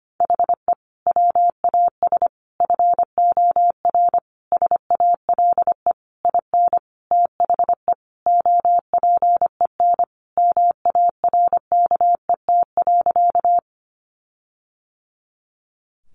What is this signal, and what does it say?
Morse code, heard as a keyed tone, about 25 wpm, 700 Hz. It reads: HE WAS FOR SALE IN THE OPEN MARKET.
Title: Sentences from Top 1400 Words 25wpm